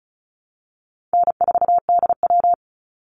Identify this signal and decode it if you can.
Morse code, heard as a keyed tone, about 34 wpm, 700 Hz. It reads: N4DW
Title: Easy Call Signs Repeated 3x - 35wpm